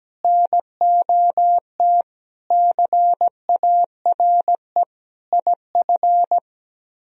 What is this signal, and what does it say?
Morse code, heard as a keyed tone, about 17 wpm, 700 Hz. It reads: NOT CARE IF
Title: Speed-Racing - Sets of 5 Words 17wpm